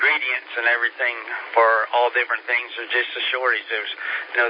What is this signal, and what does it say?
Emergency Alert System (SAME) tones, heard as harsh buzzing bursts, with speech. It is heard from 1.0 to 3.9 s.